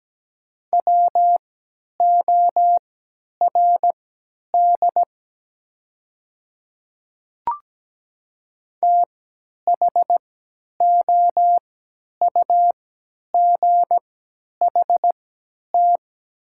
Morse code, heard as a keyed tone, about 17 wpm, spaced as for 10 wpm, 700 Hz. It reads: WORD THOUGHT
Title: 3x Character Spacing - Top 300 Words - 17wpm